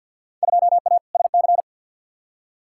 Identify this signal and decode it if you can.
Morse code, heard as a keyed tone, about 53 wpm, 700 Hz. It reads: 2A SC
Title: Field Day Exchanges - 50wpm